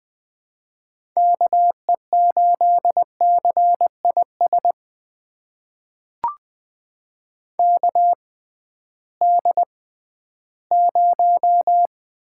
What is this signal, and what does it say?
Morse code, heard as a keyed tone, about 20 wpm, spaced as for 13 wpm, 700 Hz. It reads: KE8CIS KD0
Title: USA Call Signs - 6 Characters - 6x Character Spacing - 20wpm